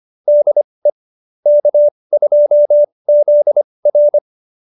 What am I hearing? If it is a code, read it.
Morse code, heard as a keyed tone, about 25 wpm, spaced as for 20 wpm, 590 Hz. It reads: DE K2ZR
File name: QSO302 20